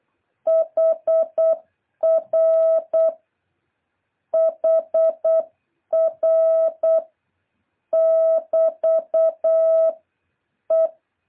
Morse code, heard as a keyed tone, about 8 wpm, 650 Hz. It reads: HR HR <BT> E
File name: jh8cbh / 2023cw